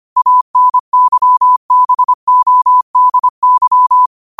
Morse code, heard as a keyed tone, about 25 wpm, 1000 Hz. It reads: ANYBODY
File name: qotdfiles / 25wpm